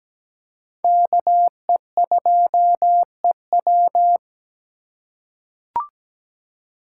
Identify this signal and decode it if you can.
Morse code, heard as a keyed tone, about 17 wpm, 700 Hz. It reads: KE2EW
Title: USA Call Signs - 8x Character Spacing - 17wpm